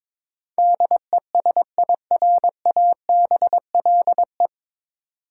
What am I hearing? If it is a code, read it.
Morse code, heard as a keyed tone, about 22 wpm, 700 Hz. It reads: DESIRABLE
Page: Morse Code Ninja practice 22wpm